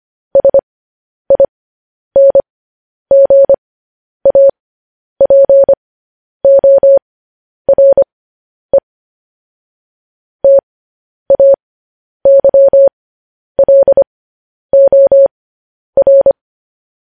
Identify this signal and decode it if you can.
Morse code, heard as a keyed tone, about 25 wpm, spaced as for 10 wpm, 550 Hz. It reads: SINGAPORE TAYLOR